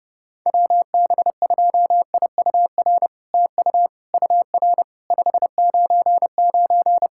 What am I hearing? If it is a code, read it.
Morse code, heard as a keyed tone, about 30 wpm, 700 Hz. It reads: WB2IUR TU UR 599
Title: SOTA QSO Practice 30wpm